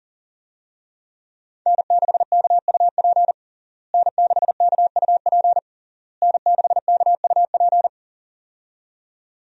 Morse code, heard as a keyed tone, about 37 wpm, 700 Hz. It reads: N6KUP N6KUP N6KUP
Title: USA Call Signs - 5 Characters - Repeated 3x - 40wpm